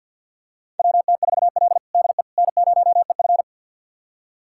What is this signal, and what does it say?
Morse code, heard as a keyed tone, about 49 wpm, 700 Hz. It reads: WT4L DE N0EF
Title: USA Callsign DE Callsign - 4 Characters - 50wpm